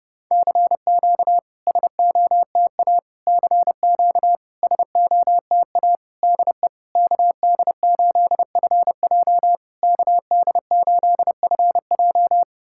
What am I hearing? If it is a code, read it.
Morse code, heard as a keyed tone, about 29 wpm, 690 Hz. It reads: CQ SOTA CQ SOTA DE KD8FJ KD8FJ
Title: SOTA QSO Practice 30wpm